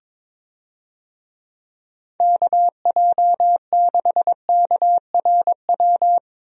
Morse code, heard as a keyed tone, about 22 wpm, 700 Hz. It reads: KJ6KRW
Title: USA Call Signs - 6 Characters - Repeated 3x - 22wpm